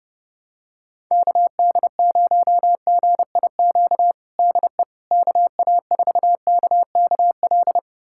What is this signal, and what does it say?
Morse code, heard as a keyed tone, about 30 wpm, 700 Hz. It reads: KD0GIQ DE KA4KKL
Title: USA Callsign DE Callsign - 6 Characters - 30wpm